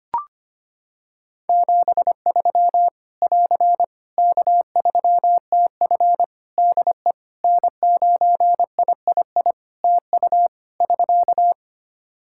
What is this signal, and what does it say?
Morse code, heard as a keyed tone, about 25 wpm, 700 Hz. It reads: 73 <AR> K3TF DE N9III TU <SK>
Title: Practice QSO - Exchange 4 - Ending - 25wpm